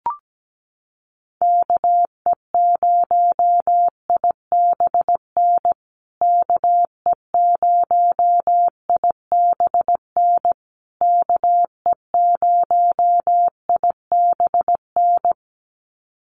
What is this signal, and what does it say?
Morse code, heard as a keyed tone, about 17 wpm, 700 Hz. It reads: KE0IBN KE0IBN KE0IBN
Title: USA Call Signs Repeated 3x 17wpm